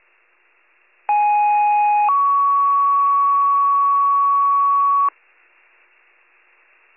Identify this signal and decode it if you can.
Two-tone page, heard as a one-second tone, then a three-second tone, about 830 then 1120 Hz. It is heard from 1.1 to 5.1 s.